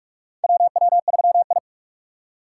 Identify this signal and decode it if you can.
Morse code, heard as a keyed tone, about 45 wpm, 700 Hz. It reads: WW3I
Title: USA Call Signs - 4 Characters - 8x Character Spacing - 45wpm